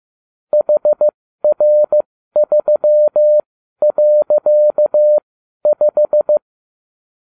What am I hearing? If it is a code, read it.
Morse code, heard as a keyed tone, about 15 wpm, spaced as for 12 wpm, 600 Hz. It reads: HR3.5